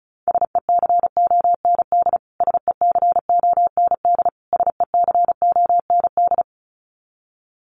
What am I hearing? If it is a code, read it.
Morse code, heard as a keyed tone, about 34 wpm, 700 Hz. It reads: SECOND SECOND SECOND
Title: Top 300 Words Repeated 35wpm